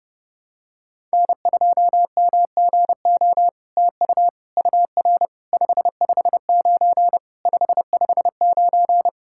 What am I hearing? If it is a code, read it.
Morse code, heard as a keyed tone, about 30 wpm, 700 Hz. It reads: N2MGO TU UR 559 559
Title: SOTA QSO Practice 30wpm